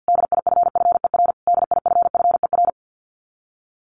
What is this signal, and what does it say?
Morse code, heard as a keyed tone, about 50 wpm, 700 Hz. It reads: DIFFER DIFFER
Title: Top 200 Words Repeated 50wpm